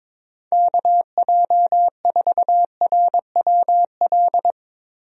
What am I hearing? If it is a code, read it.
Morse code, heard as a keyed tone, about 22 wpm, 700 Hz. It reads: KJ4RWL
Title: USA Call Signs - 6 Characters - Repeated 3x - 22wpm